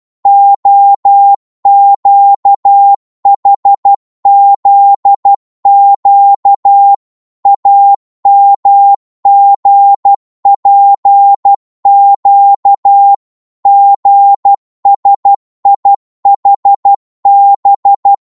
Morse code, heard as a keyed tone, about 12 wpm, 800 Hz. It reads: OQHZQ AMGPQ GSIHB
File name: Morse / ltr-12wpm